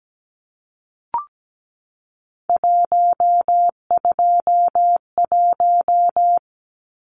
Morse code, heard as a keyed tone, about 17 wpm, 700 Hz. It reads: 121
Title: Numbers - 3-Digit 17wpm